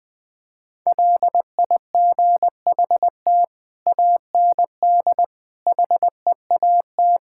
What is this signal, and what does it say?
Morse code, heard as a keyed tone, about 21 wpm, 700 Hz. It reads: LIGHT AND HEAT
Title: Sentences from Top 1000 Words - Review Part 5 - 20wpm